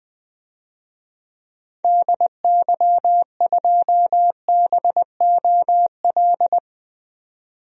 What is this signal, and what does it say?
Morse code, heard as a keyed tone, about 20 wpm, 700 Hz. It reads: DY2BOL
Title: DXCC-100 Call Signs - 6 Characters - 20wpm